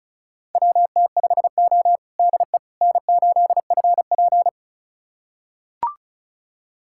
Morse code, heard as a keyed tone, about 36 wpm, 700 Hz. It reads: WT5O DE N8FP
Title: USA Callsign DE Callsign - 4 Characters - 35wpm